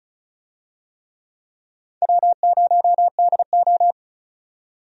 Morse code, heard as a keyed tone, about 36 wpm, 700 Hz. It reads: W0DO
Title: USA Call Signs - 4 Characters - 35wpm